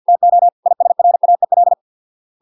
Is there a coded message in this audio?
Yes: Morse code from 0.1 to 1.7 s, heard as a keyed tone, about 50 wpm, 700 Hz. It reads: TO ISRAEL